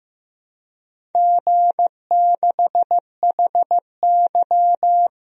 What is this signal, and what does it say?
Morse code, heard as a keyed tone, about 15 wpm, 700 Hz. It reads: G6HY
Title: DXCC-100 Call Signs - 4 Characters - Repeated 2x - 15wpm